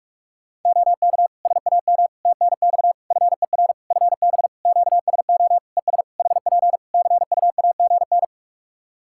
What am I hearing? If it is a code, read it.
Morse code, heard as a keyed tone, about 44 wpm, 700 Hz. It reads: OK SAM TNX FER FB QSO ES HP CUAGN
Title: Practice QSO - Exchange 4 - Ending - 45wpm